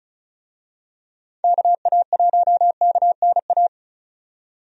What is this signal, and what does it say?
Morse code, heard as a keyed tone, about 34 wpm, 700 Hz. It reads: KA1KNA
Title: USA Call Signs - 6 Characters - 35wpm